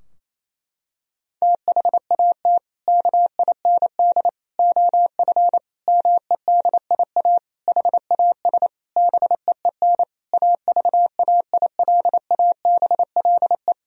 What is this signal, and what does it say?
Morse code, heard as a keyed tone, about 28 wpm, 700 Hz. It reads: THAT KIND OF MEDIA HAS BEEN AVAILABLE
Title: Sentences from Top 1000 Words - Review Part 4 - 28wpm